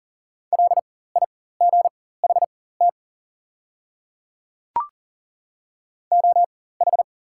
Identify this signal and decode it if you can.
Morse code, heard as a keyed tone, about 38 wpm, spaced as for 18 wpm, 690 Hz. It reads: LIGHT OH
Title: 4x Character Spacing - Top 500 Words - 40wpm